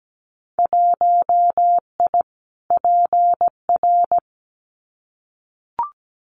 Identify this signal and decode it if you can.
Morse code, heard as a keyed tone, about 17 wpm, 700 Hz. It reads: 1I PR
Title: Winter Field Day Exchanges Encore 17wpm